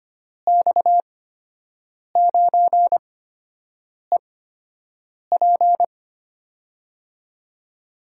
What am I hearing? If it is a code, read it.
Morse code, heard as a keyed tone, about 25 wpm, spaced as for 7 wpm, 700 Hz. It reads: X9EP
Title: USA Call Signs - 8x Character Spacing - 25wpm